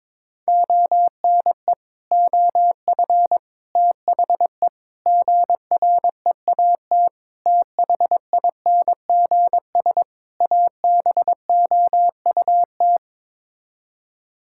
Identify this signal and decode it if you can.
Morse code, heard as a keyed tone, about 21 wpm, 700 Hz. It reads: ONE OF THE GREAT THINGS ABOUT
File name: practice / 22wpm / Sets of 6 Words - 22wpm